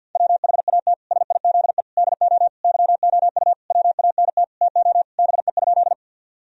Morse code, heard as a keyed tone, about 50 wpm, 700 Hz. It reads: WHAT SIZE DO YOU WANT TO BE?
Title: Sentences from Top 1000 Words - Review Part 2 - 50wpm